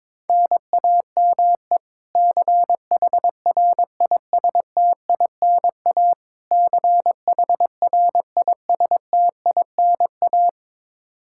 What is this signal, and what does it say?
Morse code, heard as a keyed tone, about 22 wpm, 700 Hz. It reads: NAME CHRISTINA CHRISTINA
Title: K1USN SST QSO Practice 22wpm